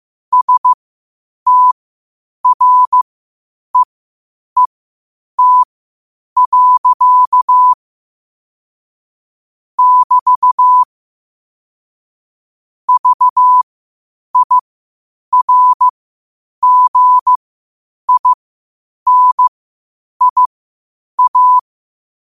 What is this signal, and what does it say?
Morse code, heard as a keyed tone, about 15 wpm, spaced as for 8 wpm, 1000 Hz. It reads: STREET. <BT> VIRGINIA
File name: qotdfiles / 8wpm